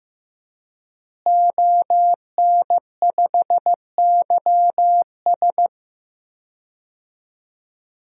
Morse code, heard as a keyed tone, about 15 wpm, 700 Hz. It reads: ON5YS
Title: DXCC-100 Call Signs - 5 Characters - Repeated 3x - 15wpm